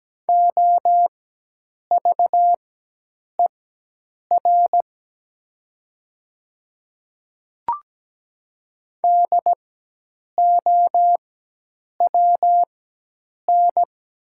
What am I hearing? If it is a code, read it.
Morse code, heard as a keyed tone, about 17 wpm, spaced as for 8 wpm, 700 Hz. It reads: OVER DOWN
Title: 4x Character Spacing - Top 300 Words - 17wpm